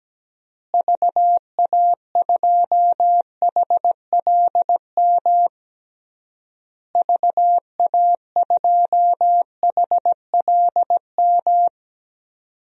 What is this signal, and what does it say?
Morse code, heard as a keyed tone, about 17 wpm, 700 Hz. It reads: VA2HLM VA2HLM
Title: DXCC-100 Call Signs - 6 Characters - Repeated 3x - 17wpm